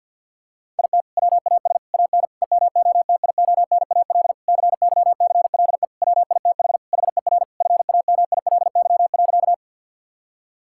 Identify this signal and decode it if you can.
Morse code, heard as a keyed tone, about 50 wpm, 700 Hz. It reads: IT WAS AN EMOTIONAL CYCLE WITH HER FAMILY.
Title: Sentences from Top 1000 Words - Review Part 6 - 50wpm